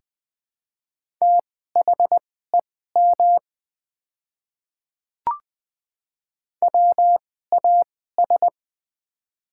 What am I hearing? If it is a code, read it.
Morse code, heard as a keyed tone, about 20 wpm, spaced as for 14 wpm, 700 Hz. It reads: THEM WAS
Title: 2x Character Spacing - Top 100 Words - 20wpm